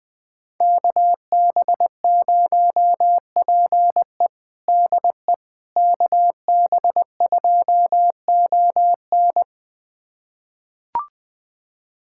Morse code, heard as a keyed tone, about 20 wpm, 700 Hz. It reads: KB0PE DE KB2ON
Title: USA Callsign DE Callsign - 5 Characters - 20wpm